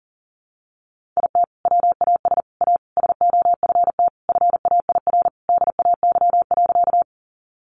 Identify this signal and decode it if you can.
Morse code, heard as a keyed tone, about 40 wpm, 700 Hz. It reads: IT WAS A SOFT FAIR DAY.
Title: Sentences from Top 1500 Words 40wpm